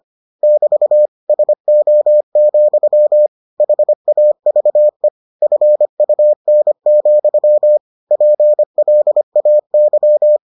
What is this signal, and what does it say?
Morse code, heard as a keyed tone, about 25 wpm, 600 Hz. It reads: <BT> SO, HAVE FUN, PLAY